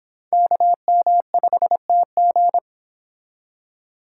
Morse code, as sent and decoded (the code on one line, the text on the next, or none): -.- -- ..... - --.
KM5TG